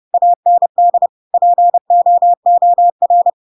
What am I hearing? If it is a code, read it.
Morse code, heard as a keyed tone, about 30 wpm, 700 Hz. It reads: AND POOR
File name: Pinocchio / Pinocchio ch7 30 30 ik1hge